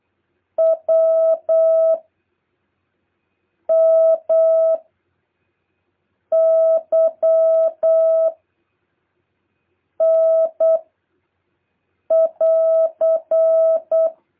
Morse code, heard as a keyed tone, about 8 wpm, 650 Hz. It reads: W M Y N <AR>
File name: jh8cbh / 2023cw